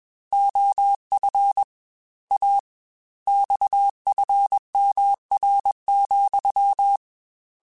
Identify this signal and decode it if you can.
Morse code, heard as a keyed tone, about 21 wpm, 800 Hz. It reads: OF A XFMR,
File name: Technical Subjects / Technical Subjects-mp3s / Sw Pwr Sup @20